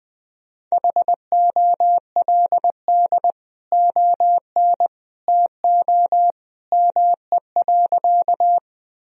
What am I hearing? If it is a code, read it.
Morse code, heard as a keyed tone, about 21 wpm, 700 Hz. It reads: HOLD ON TO ME.
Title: Sentences from Top 1000 Words - Review Part 2 - 20wpm